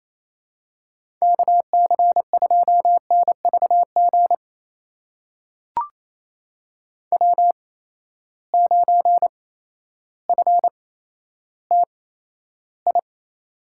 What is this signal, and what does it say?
Morse code, heard as a keyed tone, about 28 wpm, spaced as for 13 wpm, 700 Hz. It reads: KC2NVG W9FTI
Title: USA Call Signs - 8x Character Spacing - 28wpm